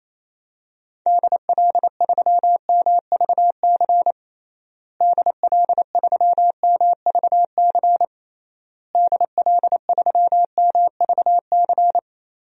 Morse code, heard as a keyed tone, about 27 wpm, 700 Hz. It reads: DL3MVC DL3MVC DL3MVC
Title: DXCC-100 Call Signs - 6 Characters - Repeated 3x - 28wpm